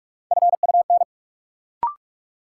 Morse code, heard as a keyed tone, about 46 wpm, 700 Hz. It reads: FUN
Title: Sentences from Top 700 Words - New Word Drill - 45wpm